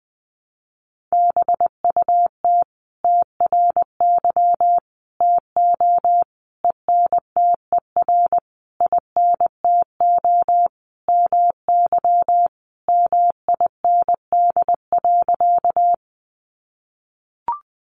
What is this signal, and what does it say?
Morse code, heard as a keyed tone, about 19 wpm, 700 Hz. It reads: BUT TRY TO ENTER INTO MY MIND.
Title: Sentences from Top 1300 Words 20wpm